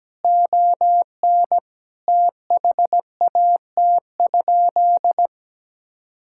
Morse code, heard as a keyed tone, about 17 wpm, 700 Hz. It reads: ON THAT?
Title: Sentences from Top 1000 Words - Review Part 5 - 17wpm